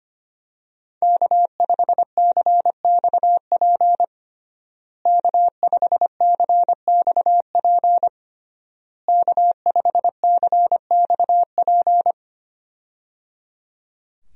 Morse code, heard as a keyed tone, about 25 wpm, 700 Hz. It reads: K5CXP K5CXP K5CXP
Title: USA Call Signs - 5 Characters - Repeated 3x - 25wpm